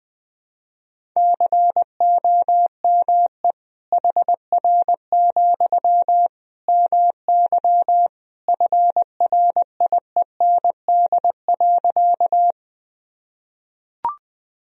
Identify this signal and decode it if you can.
Morse code, heard as a keyed tone, about 20 wpm, 700 Hz. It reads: COME HR, MY FRIEND.
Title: Sentences from Top 300 Words and Abbreviations 20wpm